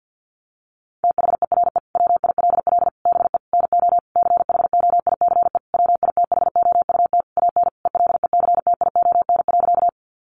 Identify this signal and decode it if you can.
Morse code, heard as a keyed tone, about 50 wpm, 700 Hz. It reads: THERE WILL BE NO CHOICE WITHOUT AN ELECTION.
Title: Sentences from Top 1500 Words 50wpm